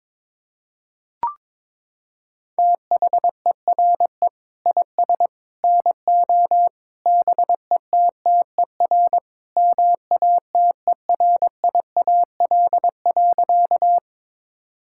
Morse code, heard as a keyed tone, about 22 wpm, 700 Hz. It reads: THERE IS NO BETTER MATERIAL.
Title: Sentences from Top 1000 Words - Review Part 5 - 22wpm